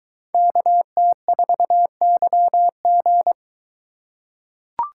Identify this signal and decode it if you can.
Morse code, heard as a keyed tone, about 22 wpm, 700 Hz. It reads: KT4YG